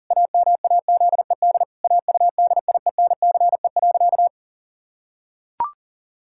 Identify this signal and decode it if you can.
Morse code, heard as a keyed tone, about 37 wpm, 700 Hz. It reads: AMAZED AUDIENCE.
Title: Pangram Challenge v2 - 40wpm